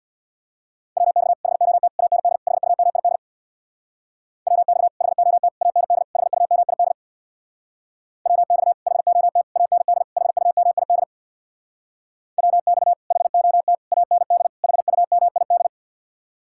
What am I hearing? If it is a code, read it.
Morse code, heard as a keyed tone, about 64 wpm, 700 Hz. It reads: WX HOT AND HUMID WX HOT AND HUMID WX HOT AND HUMID WX HOT AND HUMID